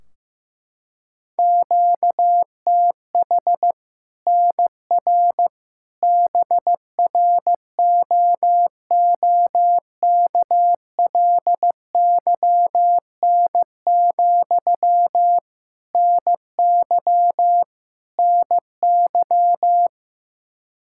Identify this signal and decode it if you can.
Morse code, heard as a keyed tone, about 16 wpm, 700 Hz. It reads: QTH NR BROOKLYN, NY NY
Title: Practice QSO - Exchange 1 - Beginning - 15wpm